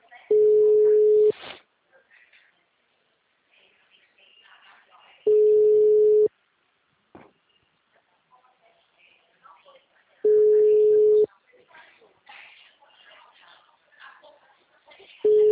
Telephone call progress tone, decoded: ringback tone

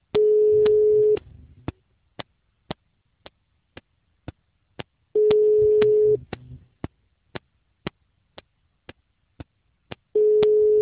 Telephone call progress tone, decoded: ringback tone

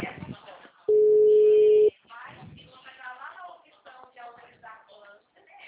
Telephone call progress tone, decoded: ringback tone

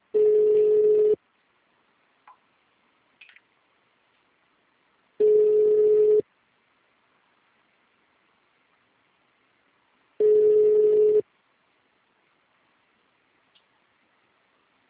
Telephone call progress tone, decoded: ringback tone

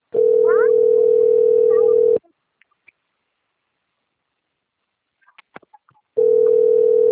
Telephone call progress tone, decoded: ringback tone

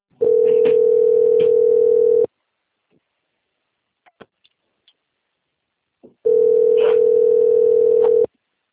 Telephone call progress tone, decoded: ringback tone

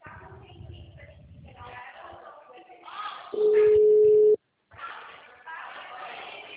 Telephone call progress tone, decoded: ringback tone

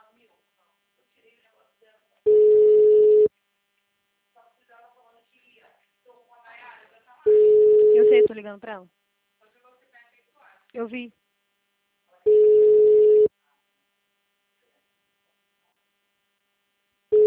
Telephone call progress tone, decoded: ringback tone

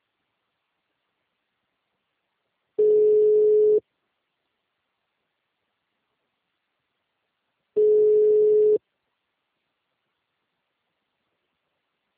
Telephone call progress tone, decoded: ringback tone

